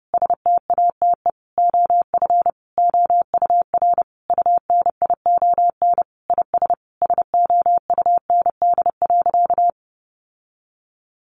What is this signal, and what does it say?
Morse code, heard as a keyed tone, about 30 wpm, 700 Hz. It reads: STATE OF OUR UNION IS SOUND.
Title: Sentences from Top 1400 Words 30wpm